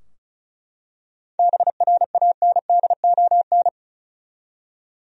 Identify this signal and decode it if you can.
Morse code, heard as a keyed tone, about 34 wpm, 700 Hz. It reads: BRANDON
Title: USA Common Names - 35wpm